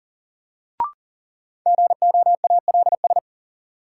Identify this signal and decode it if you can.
Morse code, heard as a keyed tone, about 40 wpm, 700 Hz. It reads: GOALS
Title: Sentences from Top 1000 Words - Review Part 6 - New Word Drill - 40wpm